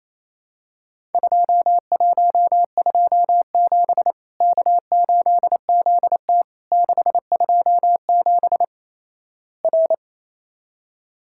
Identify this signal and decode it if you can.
Morse code, heard as a keyed tone, about 28 wpm, 700 Hz. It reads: 2127 K8ZT 627 R
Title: Worked All Europe DX Contest - QTC Exchange - 28wpm